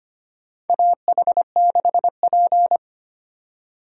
Morse code, heard as a keyed tone, about 25 wpm, 700 Hz. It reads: AH6P
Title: USA Call Signs - 4 Characters - 25wpm